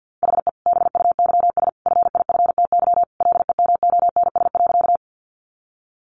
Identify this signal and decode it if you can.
Morse code, heard as a keyed tone, about 50 wpm, 700 Hz. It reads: HE BUYS FIFTY LEMONS.
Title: Additional Easy Phrases 50wpm